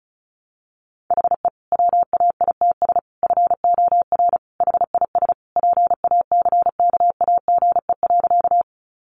Morse code, heard as a keyed tone, about 34 wpm, 700 Hz. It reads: HE WAITS FOR HIS PACKAGE.
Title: Additional Easy Phrases 35wpm